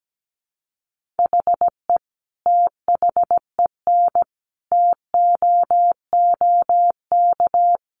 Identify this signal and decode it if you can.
Morse code, heard as a keyed tone, about 17 wpm, 700 Hz. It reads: HE THEN TOOK
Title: Sentences from Top 1400 Words 17wpm